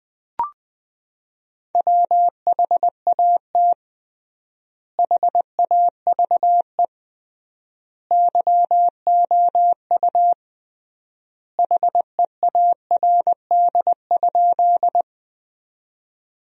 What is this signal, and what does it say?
Morse code, heard as a keyed tone, about 20 wpm, 700 Hz. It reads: WHAT HAVE YOU HEARD?
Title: 3x Word-Spacing - Sentences from Top 400 Words - 20wpm